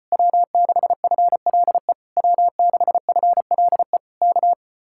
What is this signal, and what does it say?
Morse code, heard as a keyed tone, about 33 wpm, 700 Hz. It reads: W6FLE W6FLE K